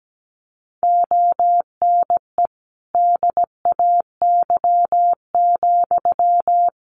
Morse code, heard as a keyed tone, about 17 wpm, 700 Hz. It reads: ONE DAY,
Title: Sentences from Top 1300 Words 17wpm